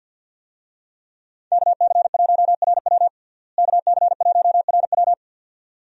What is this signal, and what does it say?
Morse code, heard as a keyed tone, about 52 wpm, 700 Hz. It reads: KC1RW KC1RW
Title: USA Call Signs - 5 Characters - Repeated 2x - 50wpm